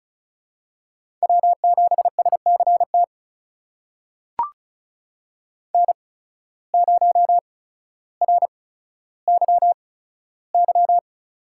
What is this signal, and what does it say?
Morse code, heard as a keyed tone, about 36 wpm, spaced as for 15 wpm, 700 Hz. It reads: W7SCT N0RYY
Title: USA Call Signs - 5 Characters - 8x Character Spacing - 35wpm